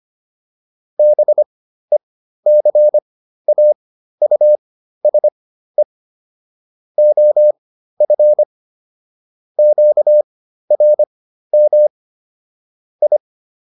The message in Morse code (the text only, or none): BECAUSE OF QRM I